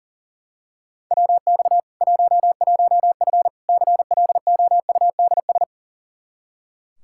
Morse code, heard as a keyed tone, about 40 wpm, 700 Hz. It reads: WX 11F CLOUDS
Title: Practice QSO - Exchange 2 - Info - 40wpm